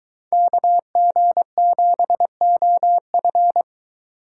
Morse code, heard as a keyed tone, about 23 wpm, 700 Hz. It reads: KG7OF